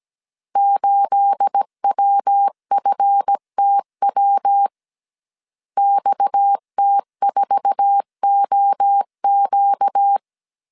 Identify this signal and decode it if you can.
Morse code, heard as a keyed tone, about 17 wpm, 800 Hz. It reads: 8WFTW XT4OQ
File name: CW / DET / 1 9 1500C S A 16wpm 2828